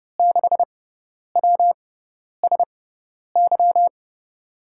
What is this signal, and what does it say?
Morse code, heard as a keyed tone, about 30 wpm, spaced as for 10 wpm, 700 Hz. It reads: 6WSY
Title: USA Call Signs - 6 Characters - 6x Character Spacing - 30wpm